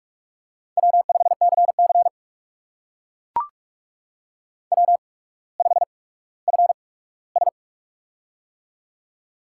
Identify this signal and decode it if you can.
Morse code, heard as a keyed tone, about 44 wpm, 690 Hz. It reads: W 5 C C W 5 F S
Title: USA Call Signs - 8x Character Spacing - 45wpm